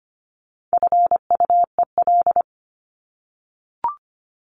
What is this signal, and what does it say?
Morse code, heard as a keyed tone, about 25 wpm, 700 Hz. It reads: FUEL
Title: Top 1751-2000 Words 25wpm